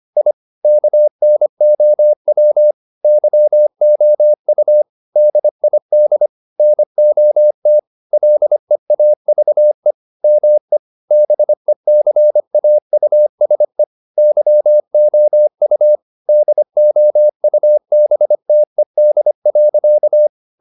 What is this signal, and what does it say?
Morse code, heard as a keyed tone, about 25 wpm, 600 Hz. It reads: I KNOW YOU DID NOT LEAVE ME BECAUSE YOU DOUBTED.
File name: the old man and the sea part 1 25wpm